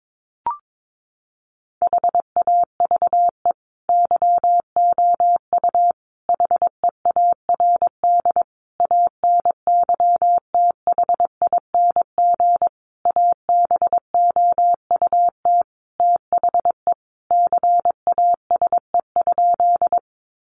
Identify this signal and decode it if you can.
Morse code, heard as a keyed tone, about 22 wpm, 700 Hz. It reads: HAVE YOU HEARD ANYTHING ABOUT THE CASE?
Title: Sentences from Top 1000 Words 22wpm